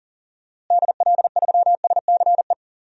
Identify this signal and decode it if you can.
Morse code, heard as a keyed tone, about 40 wpm, 700 Hz. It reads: DL3SCE
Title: DXCC-100 Call Signs - 6 Characters - Repeated 3x - 40wpm